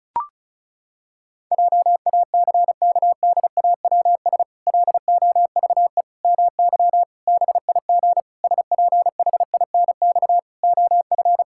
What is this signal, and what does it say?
Morse code, heard as a keyed tone, about 36 wpm, 700 Hz. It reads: JACKDAWS LOVE MY BIG SPHINX OF
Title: Pangram Challenge v2 - 35wpm